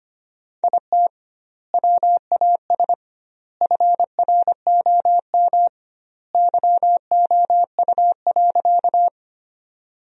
Morse code, heard as a keyed tone, about 25 wpm, 700 Hz. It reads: IT WAS FROM YOU.
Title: 2x Word-Spacing - Sentences from Top 100 Words - 25wpm